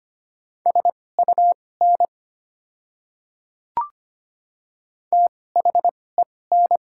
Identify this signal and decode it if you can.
Morse code, heard as a keyed tone, about 25 wpm, spaced as for 18 wpm, 690 Hz. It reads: SUN THEN
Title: 2x Character Spacing - Top 300 Words - 25wpm